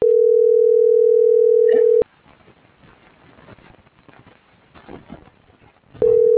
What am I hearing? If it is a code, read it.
Ringback tone (North America), 440 + 480 Hz.
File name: CENTRAL GENERAL / out yarit.rdz - .64515